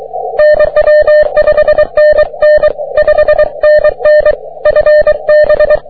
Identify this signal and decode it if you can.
Morse code, heard as a keyed tone, about 23 wpm, 600 Hz. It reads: NW 5NN 5NN FB
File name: wp / jg3qzn 30m cw